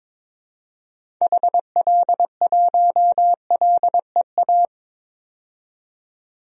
Morse code, heard as a keyed tone, about 21 wpm, 700 Hz. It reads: HL1LEA
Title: DXCC-100 Call Signs - 6 Characters - 22wpm